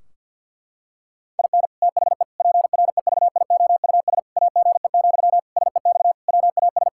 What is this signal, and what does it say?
Morse code, heard as a keyed tone, about 50 wpm, 700 Hz. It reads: IN THE PREVIOUS AGE, SEX WAS
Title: Sentences from Top 1600 Words 50wpm